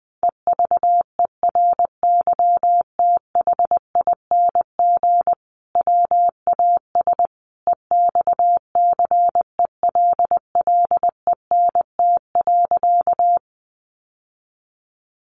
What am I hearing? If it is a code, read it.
Morse code, heard as a keyed tone, about 20 wpm, 700 Hz. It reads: EVERYTHING WAS EXCELLENT.
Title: Sentences from Top 1000 Words 20wpm